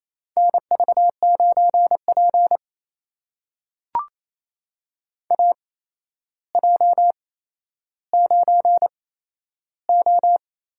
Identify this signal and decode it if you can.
Morse code, heard as a keyed tone, about 28 wpm, spaced as for 12 wpm, 700 Hz. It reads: NV9P AJ9O
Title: USA Call Signs - 4 Characters - 8x Character Spacing - 28wpm